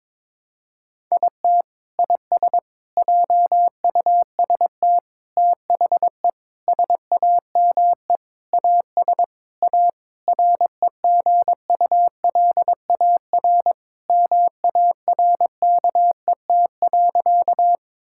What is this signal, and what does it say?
Morse code, heard as a keyed tone, about 22 wpm, 700 Hz. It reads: IT IS JUST THE SAME AS A REGULAR MARKET.
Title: Sentences from Top 1000 Words - Review Part 6 - 22wpm